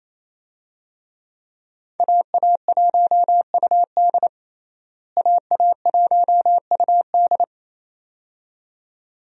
Morse code, heard as a keyed tone, about 28 wpm, 700 Hz. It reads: AA1UD AA1UD
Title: USA Call Signs - 5 Characters - Repeated 2x - 28wpm